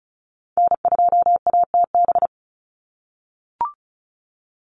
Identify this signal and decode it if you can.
Morse code, heard as a keyed tone, about 34 wpm, 700 Hz. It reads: N2ATB
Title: Call Signs - 35wpm